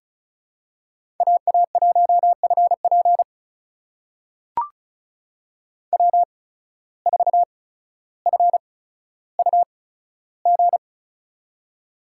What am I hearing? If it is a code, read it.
Morse code, heard as a keyed tone, about 36 wpm, spaced as for 15 wpm, 700 Hz. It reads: AA1FP W4FUG
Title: USA Call Signs - 5 Characters - 8x Character Spacing - 35wpm